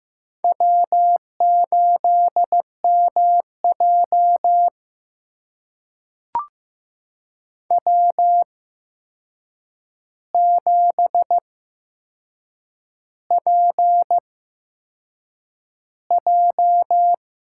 Morse code, heard as a keyed tone, about 15 wpm, spaced as for 6 wpm, 700 Hz. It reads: W8MJ W7PJ
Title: USA Call Signs - 4 Characters - 8x Character Spacing - 15wpm